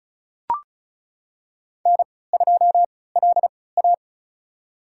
Morse code, heard as a keyed tone, about 34 wpm, spaced as for 20 wpm, 700 Hz. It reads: N2LA
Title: USA Call Signs - 4 Characters - 3x Character Spacing - 35wpm